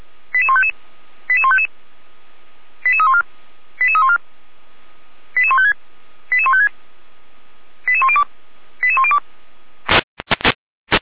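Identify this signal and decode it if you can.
Five-tone selective call (ZVEI), heard as quick runs of five tones, tones of about 70 ms. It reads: 88155 88155 88314 88314 88156 88156 88102 88102
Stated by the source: ZVEI five-tone call received by SDR.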